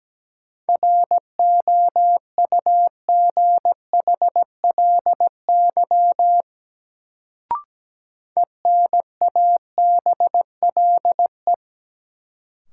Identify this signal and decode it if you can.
Morse code, heard as a keyed tone, about 17 wpm, 700 Hz. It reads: ROUGHLY ENABLE
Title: Sentences from Top 1500 Words - New Word Drill Alt - 17wpm